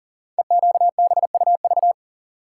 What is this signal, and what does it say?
Morse code, heard as a keyed tone, about 38 wpm, 700 Hz. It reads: EQBUV
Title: Code Groups - 5 Letters - 40wpm